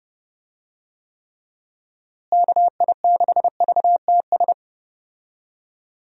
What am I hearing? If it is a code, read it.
Morse code, heard as a keyed tone, about 30 wpm, 700 Hz. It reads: KI6VTS
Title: USA Call Signs - 6 Characters - Repeated 2x - 30wpm